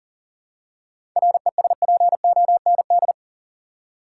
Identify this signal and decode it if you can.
Morse code, heard as a keyed tone, about 40 wpm, 700 Hz. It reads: RESPOND
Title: Seven Letter Words - 40wpm